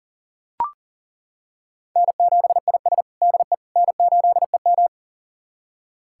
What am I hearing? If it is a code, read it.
Morse code, heard as a keyed tone, about 40 wpm, 700 Hz. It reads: N7IS DE N8EM
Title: USA Callsign DE Callsign - 4 Characters - 40wpm